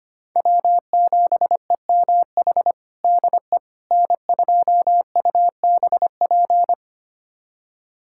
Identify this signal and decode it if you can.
Morse code, heard as a keyed tone, about 25 wpm, 700 Hz. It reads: W7EMH DE N2UBP
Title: USA Callsign DE Callsign - 5 Characters - 25wpm